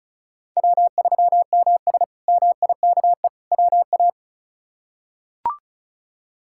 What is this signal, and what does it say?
Morse code, heard as a keyed tone, about 36 wpm, 700 Hz. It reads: W3MS MIKE WA
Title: K1USN SST QSO Practice - 35wpm